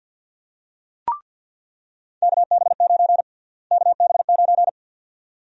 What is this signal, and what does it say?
Morse code, heard as a keyed tone, about 49 wpm, 700 Hz. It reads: KB9 KB9
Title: USA Call Sign Prefixes - Repeated 2x - 50wpm